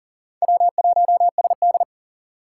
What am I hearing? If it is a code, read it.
Morse code, heard as a keyed tone, about 40 wpm, 700 Hz. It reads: W1SD
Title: USA Call Signs - 4 Characters - 4x Character Spacing - 40wpm